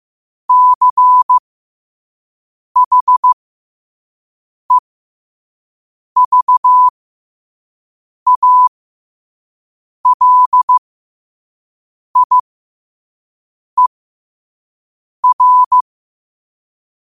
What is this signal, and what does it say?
Morse code, heard as a keyed tone, about 15 wpm, spaced as for 5 wpm, 1000 Hz. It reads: CHEVALIER
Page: qotd-at-5-wpm